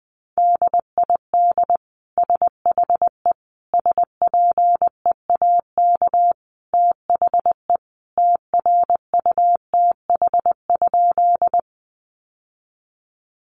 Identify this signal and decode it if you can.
Morse code, heard as a keyed tone, about 20 wpm, 700 Hz. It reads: DID SHE SPEAK THE TRUTH?
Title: Sentences from Top 1100 Words 20wpm